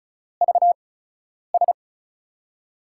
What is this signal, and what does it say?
Morse code, heard as a keyed tone, about 36 wpm, spaced as for 10 wpm, 700 Hz. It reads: VS
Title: USA Call Signs - 8x Character Spacing - 35wpm